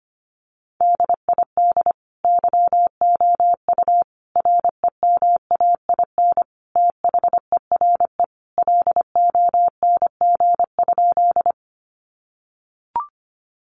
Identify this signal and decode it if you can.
Morse code, heard as a keyed tone, about 25 wpm, 700 Hz. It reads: DID YOU REMAIN THERE LONG?
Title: Sentences from Top 1200 Words 25wpm